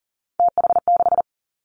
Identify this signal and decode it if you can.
Morse code, heard as a keyed tone, about 38 wpm, 700 Hz. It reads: TH6
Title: Three Challenging Letter-Number Combinations 40wpm